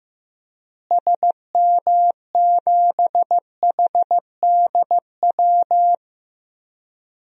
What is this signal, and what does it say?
Morse code, heard as a keyed tone, about 16 wpm, 700 Hz. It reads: SM7HDW
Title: DXCC-100 Call Signs - 6 Characters - Repeated 2x - 15wpm